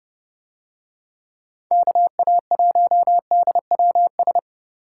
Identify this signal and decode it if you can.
Morse code, heard as a keyed tone, about 28 wpm, 700 Hz. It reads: KA1DWS